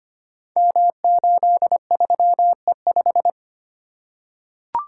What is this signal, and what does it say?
Morse code, heard as a keyed tone, about 25 wpm, 700 Hz. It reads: M83E5
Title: Code Groups - 5 Letter:Numbers - 25wpm